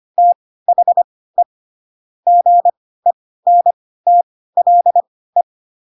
Morse code, heard as a keyed tone, about 25 wpm, spaced as for 16 wpm, 700 Hz. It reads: THE GENTLE
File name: Pinocchio / Pinocchio ch2 10 25 ik1hge